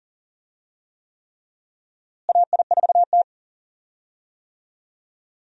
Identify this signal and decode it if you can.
Morse code, heard as a keyed tone, about 40 wpm, 700 Hz. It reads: AI4T
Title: USA Call Signs - 4 Characters - Repeated 3x - 40wpm